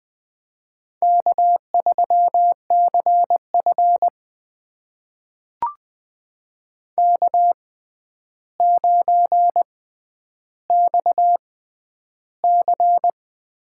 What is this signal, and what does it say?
Morse code, heard as a keyed tone, about 20 wpm, spaced as for 10 wpm, 700 Hz. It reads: K3CF K9XC
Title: USA Call Signs - 4 Characters - 6x Character Spacing - 20wpm